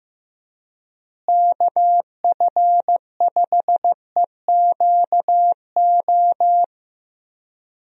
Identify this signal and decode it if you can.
Morse code, heard as a keyed tone, about 15 wpm, 700 Hz. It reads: KF5EQO
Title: USA Call Signs - 6 Characters - Repeated 2x - 15wpm